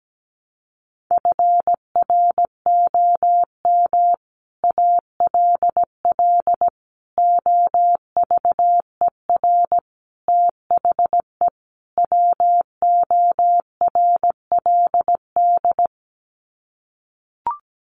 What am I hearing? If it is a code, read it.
Morse code, heard as a keyed tone, about 17 wpm, 700 Hz. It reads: FROM ALL OVER THE WORLD
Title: Sets of 5 Words 17wpm